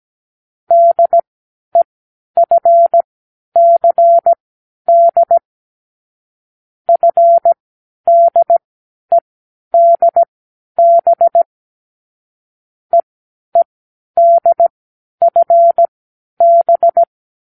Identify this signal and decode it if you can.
Morse code, heard as a keyed tone, about 17 wpm, spaced as for 10 wpm, 680 Hz. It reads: DEFCD FDEDB EEDFB
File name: k7qo / Lesson9(A-F)